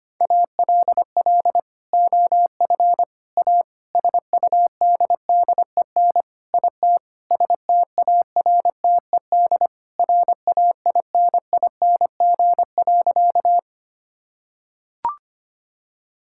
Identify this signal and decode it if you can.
Morse code, heard as a keyed tone, about 25 wpm, 700 Hz. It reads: ALL OF A SUDDEN IT STARTED RAINING.
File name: practice / 25wpm / Sentences with 4-Word Phrases - 25wpm